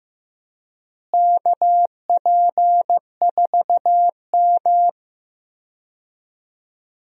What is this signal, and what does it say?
Morse code, heard as a keyed tone, about 15 wpm, 700 Hz. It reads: KP4M
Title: DXCC-100 Call Signs - 4 Characters - Repeated 2x - 15wpm